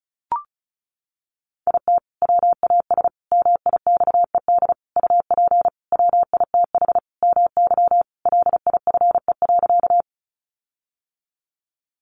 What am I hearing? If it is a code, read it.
Morse code, heard as a keyed tone, about 34 wpm, 700 Hz. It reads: IT WAS MIXED UP WITH MY LIFE.
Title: Sentences from Top 1500 Words 35wpm